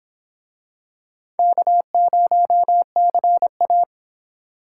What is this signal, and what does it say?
Morse code, heard as a keyed tone, about 25 wpm, 700 Hz. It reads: K0CA